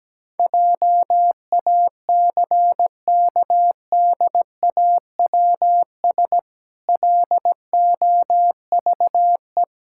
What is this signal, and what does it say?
Morse code, heard as a keyed tone, about 17 wpm, 700 Hz. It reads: JACKDAWS LOVE
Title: Pangram Challenge v2 - 17wpm